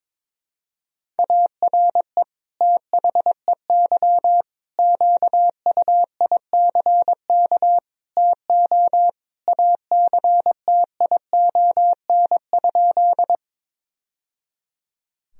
Morse code, heard as a keyed tone, about 21 wpm, 700 Hz. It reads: ARE THEY QUICK TO ACTION?
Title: Sentences from Top 1000 Words - Review Part 4 - 22wpm